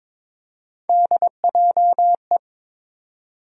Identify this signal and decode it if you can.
Morse code, heard as a keyed tone, about 22 wpm, 700 Hz. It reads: DJE
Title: USA Call Sign Postfixes - 3 Letters 22wpm